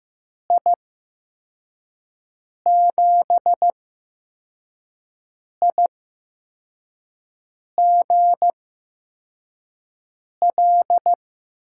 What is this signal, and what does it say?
Morse code, heard as a keyed tone, about 15 wpm, spaced as for 4 wpm, 700 Hz. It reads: I7IGL
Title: USA Call Signs - 6 Characters - 8x Character Spacing - 15wpm